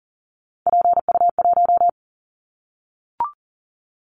Morse code, heard as a keyed tone, about 40 wpm, 700 Hz. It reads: PU1
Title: Callsign Prefixes for Top 50 DXCC - 40wpm